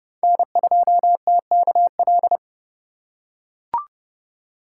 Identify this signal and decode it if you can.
Morse code, heard as a keyed tone, about 29 wpm, 700 Hz. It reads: N2TKL
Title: USA Call Signs - 5 Characters - 2x Character Spacing - 30wpm